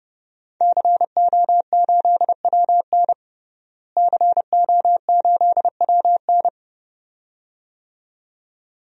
Morse code, heard as a keyed tone, about 30 wpm, 700 Hz. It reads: CO8WN CO8WN
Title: DXCC-100 Call Signs - 5 Characters - Repeated 2x - 30wpm